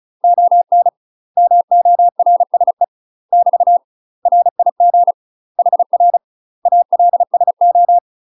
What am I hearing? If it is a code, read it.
Morse code, heard as a keyed tone, about 34 wpm, 700 Hz. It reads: ON MORSE <BT> RIG HR ALSO
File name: UK MASTER QSOs TRAINING/35WPM